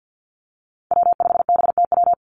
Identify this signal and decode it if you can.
Morse code, heard as a keyed tone, about 50 wpm, 700 Hz. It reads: W5BTW
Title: USA Call Signs 50wpm